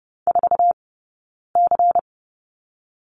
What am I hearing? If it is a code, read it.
Morse code, heard as a keyed tone, about 30 wpm, 700 Hz. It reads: 4 C
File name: Lesson 16d - Letter c - Two Characters 30wpm